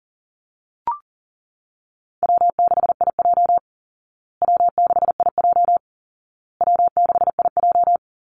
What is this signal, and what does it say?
Morse code, heard as a keyed tone, about 40 wpm, 700 Hz. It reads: W6IJ W6IJ W6IJ
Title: Call Signs Repeated 2x - 40wpm